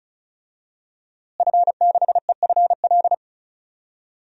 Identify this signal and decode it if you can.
Morse code, heard as a keyed tone, about 36 wpm, 700 Hz. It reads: F6EFL
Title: DXCC-100 Call Signs - 5 Characters - Repeated 2x - 35wpm